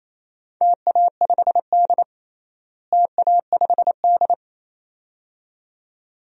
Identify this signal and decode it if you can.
Morse code, heard as a keyed tone, about 28 wpm, 700 Hz. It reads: TA5D TA5D
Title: DXCC-100 Call Signs - 4 Characters - Repeated 2x - 28wpm